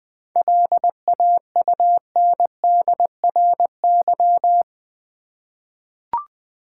Morse code, heard as a keyed tone, about 20 wpm, 700 Hz. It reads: LAUNDRY
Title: Easy Words with Hints - 20wpm